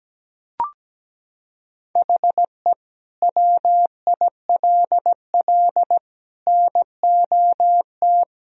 Morse code, heard as a keyed tone, about 17 wpm, 700 Hz. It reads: HE WILL NOT
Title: Sentences from Top 1000 Words - Review Part 1 - 17wpm